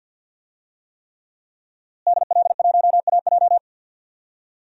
Morse code, heard as a keyed tone, about 45 wpm, 700 Hz. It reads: DL1AJ
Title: DXCC-100 Call Signs - 5 Characters - Repeated 3x - 50wpm